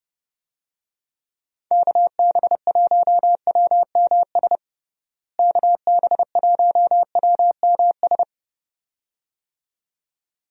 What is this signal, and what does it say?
Morse code, heard as a keyed tone, about 30 wpm, 700 Hz. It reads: KB1WMS KB1WMS
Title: USA Call Signs - 6 Characters - Repeated 2x - 30wpm